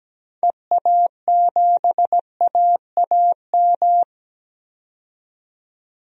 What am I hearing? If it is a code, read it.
Morse code, heard as a keyed tone, about 17 wpm, 700 Hz. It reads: EA7AAM
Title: DXCC-100 Call Signs - 6 Characters - 17wpm